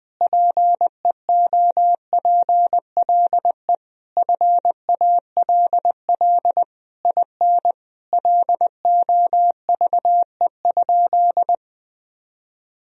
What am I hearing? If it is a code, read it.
Morse code, heard as a keyed tone, about 21 wpm, 700 Hz. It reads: PEOPLE FALL IN LOVE?
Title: Sentences from Top 1000 Words - Review Part 2 - 20wpm